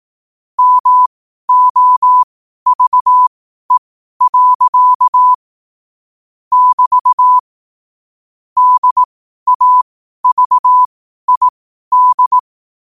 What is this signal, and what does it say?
Morse code, heard as a keyed tone, about 18 wpm, spaced as for 12 wpm, 1000 Hz. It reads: MOVE. <BT> DAVID
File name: qotdfiles / 12wpm